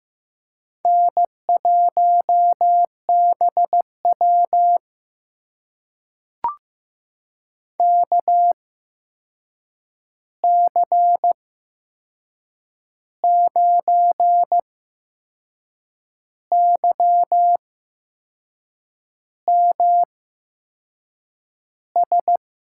Morse code, heard as a keyed tone, about 15 wpm, spaced as for 6 wpm, 700 Hz. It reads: N1BW KC9YMS
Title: USA Call Signs - 8x Character Spacing - 15wpm